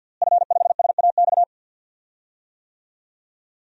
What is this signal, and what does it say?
Morse code, heard as a keyed tone, about 51 wpm, 700 Hz. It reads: F5SAX
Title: DXCC-100 Call Signs - 5 Characters - Repeated 3x - 50wpm